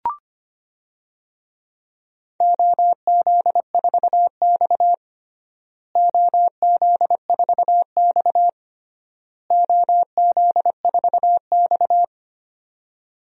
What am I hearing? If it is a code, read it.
Morse code, heard as a keyed tone, about 25 wpm, 700 Hz. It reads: OZ4X OZ4X OZ4X
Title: DXCC-100 Call Signs - 4 Characters - Repeated 3x - 25wpm